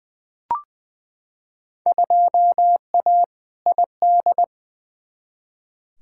Morse code, heard as a keyed tone, about 20 wpm, 700 Hz. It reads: 2A ID
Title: Field Day Exchanges - 20wpm